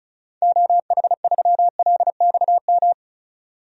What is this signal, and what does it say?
Morse code, heard as a keyed tone, about 36 wpm, 690 Hz. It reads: OH3LXM
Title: DXCC-100 Call Signs - 6 Characters - 35wpm